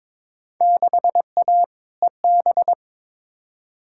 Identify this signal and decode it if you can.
Morse code, heard as a keyed tone, about 21 wpm, 700 Hz. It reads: 6A EB
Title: Field Day Exchanges Encore - 22wpm